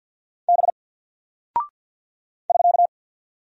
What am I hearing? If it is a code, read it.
Morse code, heard as a keyed tone, about 48 wpm, 700 Hz. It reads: B <SK>
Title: Single Character (Letter:Number:Punctuation:Prosign) 50wpm